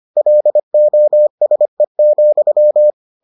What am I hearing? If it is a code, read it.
Morse code, heard as a keyed tone, about 25 wpm, 600 Hz. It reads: LOSE,